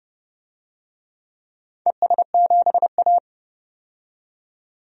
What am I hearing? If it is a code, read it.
Morse code, heard as a keyed tone, about 30 wpm, 700 Hz. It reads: ES7A
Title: DXCC-100 Call Signs - 4 Characters - 30wpm